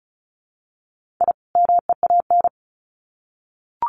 Morse code, heard as a keyed tone, about 34 wpm, 700 Hz. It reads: I MEAN
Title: Sets of 2 Words 35wpm